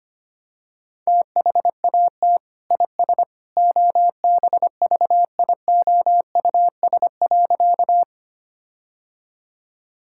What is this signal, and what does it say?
Morse code, heard as a keyed tone, about 25 wpm, 700 Hz. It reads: THAT IS OBVIOUS.
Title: Sentences from Top 1600 Words 25wpm